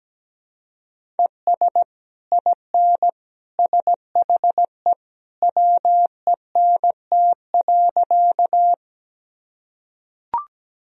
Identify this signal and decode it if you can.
Morse code, heard as a keyed tone, about 17 wpm, 700 Hz. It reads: ES IN SHE WENT.
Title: Sentences from Top 200 Words and Abbreviations 17wpm